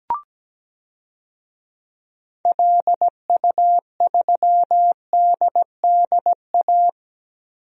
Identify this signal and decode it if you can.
Morse code, heard as a keyed tone, about 17 wpm, 700 Hz. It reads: LU3DDA
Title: DXCC-100 Call Signs - 6 Characters - Repeated 2x - 17wpm